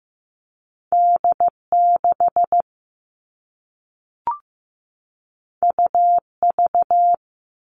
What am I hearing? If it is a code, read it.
Morse code, heard as a keyed tone, about 15 wpm, 700 Hz. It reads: D6 UV
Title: All Cut Numbers - 2-Digit 15wpm